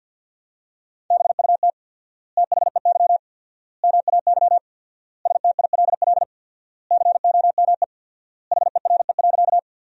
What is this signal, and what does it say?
Morse code, heard as a keyed tone, about 53 wpm, 700 Hz. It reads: BUT THEY MAY STILL COME HERE.
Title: 4x Word-Spacing - Sentences from Top 300 Words - 50wpm